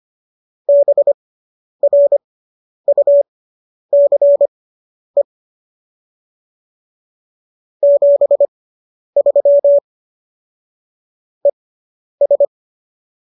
Morse code, heard as a keyed tone, about 25 wpm, spaced as for 10 wpm, 570 Hz. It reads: BRUCE 73 ES